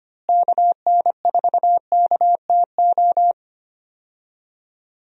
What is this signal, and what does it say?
Morse code, heard as a keyed tone, about 25 wpm, 700 Hz. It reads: KN4KTO
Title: USA Call Signs - 6 Characters - Repeated 3x - 25wpm